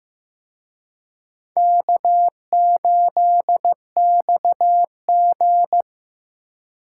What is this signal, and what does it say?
Morse code, heard as a keyed tone, about 18 wpm, 700 Hz. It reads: K8XG
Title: Speed Racing - USA 1x2 Call Signs 15wpm